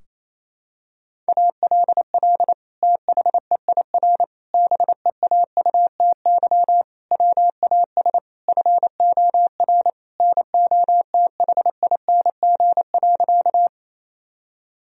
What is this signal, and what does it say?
Morse code, heard as a keyed tone, about 28 wpm, 700 Hz. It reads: ALL THEIR BEAUTY WAS FOR NOTHING.
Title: Sentences from Top 1000 Words - Review Part 2 - 28wpm